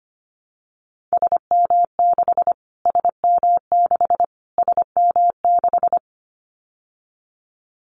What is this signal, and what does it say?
Morse code, heard as a keyed tone, about 25 wpm, 700 Hz. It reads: SM6 SM6 SM6
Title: Callsign Prefixes for Top 25 DXCC - Repeated 3x - 25wpm